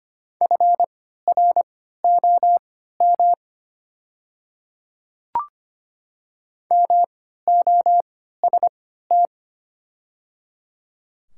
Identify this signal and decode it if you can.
Morse code, heard as a keyed tone, about 25 wpm, spaced as for 14 wpm, 700 Hz. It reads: FROM MOST
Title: 3x Character Spacing - Top 100 Words - 25wpm